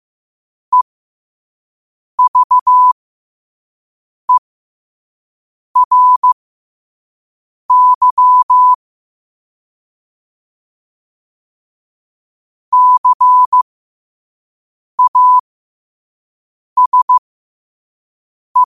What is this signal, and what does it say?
Morse code, heard as a keyed tone, about 15 wpm, spaced as for 5 wpm, 1000 Hz. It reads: EVERY CASE
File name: qotdfiles / 5wpm